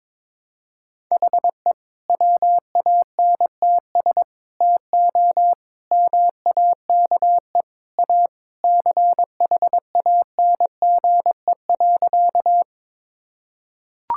Morse code, heard as a keyed tone, about 21 wpm, 700 Hz. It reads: HE WANTS TO MAKE A CHANGE.
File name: practice / 22wpm / Sentences with 3-Word Phrases - 22wpm